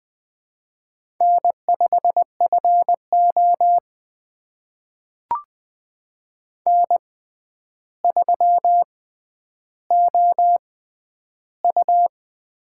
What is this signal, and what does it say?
Morse code, heard as a keyed tone, about 20 wpm, spaced as for 10 wpm, 700 Hz. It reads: N5FO N3OU
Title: USA Call Signs - 4 Characters - 6x Character Spacing - 20wpm